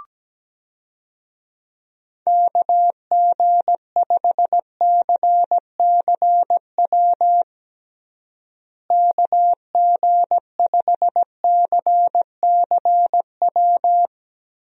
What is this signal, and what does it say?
Morse code, heard as a keyed tone, about 17 wpm, 700 Hz. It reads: KG5CCW KG5CCW
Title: USA Call Signs - 6 Characters - Repeated 3x - 17wpm